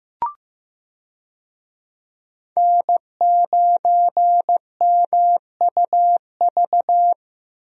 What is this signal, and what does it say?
Morse code, heard as a keyed tone, about 15 wpm, 700 Hz. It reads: N9MUV
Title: USA Call Signs - 5 Characters - Repeated 2x - 15wpm